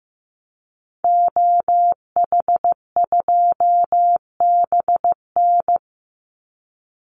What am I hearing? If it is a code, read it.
Morse code, heard as a keyed tone, about 15 wpm, 700 Hz. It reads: OH2BN
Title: Call Signs Repeated 2x - 15wpm